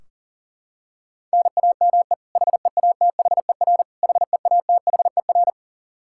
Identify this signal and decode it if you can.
Morse code, heard as a keyed tone, about 40 wpm, 700 Hz. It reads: NAME HEATHER HEATHER
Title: K1USN SST QSO Practice 40wpm